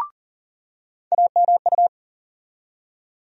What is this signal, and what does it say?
Morse code, heard as a keyed tone, about 40 wpm, 700 Hz. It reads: AMU
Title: USA Call Sign Postfixes - 3 Letters 40wpm